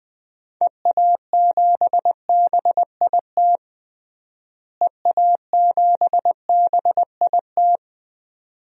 Morse code, heard as a keyed tone, about 20 wpm, 700 Hz. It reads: EA7BIT EA7BIT
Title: DXCC-100 Call Signs - 6 Characters - Repeated 2x - 20wpm